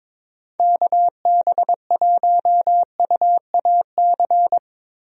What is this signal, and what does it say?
Morse code, heard as a keyed tone, about 21 wpm, 700 Hz. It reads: KB1UAC
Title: USA Call Signs - 6 Characters - Repeated 3x - 22wpm